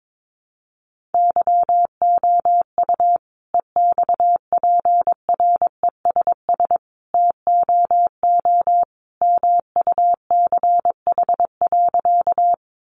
Morse code, heard as a keyed tone, about 22 wpm, 700 Hz. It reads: YOU EXPRESS TOO MUCH.
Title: Sentences from Top 1100 Words 22wpm